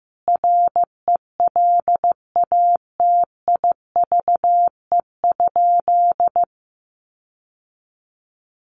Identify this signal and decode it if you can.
Morse code, heard as a keyed tone, about 15 wpm, 700 Hz. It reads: RELATIVE?
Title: Sentences from Top 1400 Words 15wpm